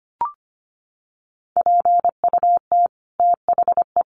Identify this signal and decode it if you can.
Morse code, heard as a keyed tone, about 25 wpm, 700 Hz. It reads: PUT THE
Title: Sentences from Top 1200 Words 25wpm